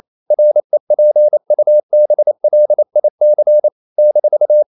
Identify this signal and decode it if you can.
Morse code, heard as a keyed tone, about 28 wpm, 600 Hz. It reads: REPUBLIC -
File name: CW / Practice / NationalCapitals28